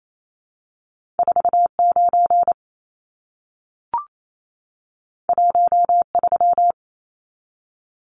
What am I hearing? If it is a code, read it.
Morse code, heard as a keyed tone, about 28 wpm, 700 Hz. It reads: 49 13
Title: Numbers - 2-Digit 28wpm